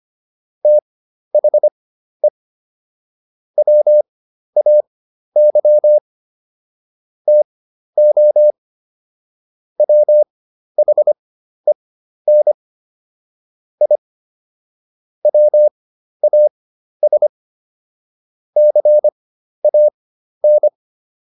Morse code, heard as a keyed tone, about 25 wpm, spaced as for 12 wpm, 600 Hz. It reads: THE WAY TO WHEN I WAS CAN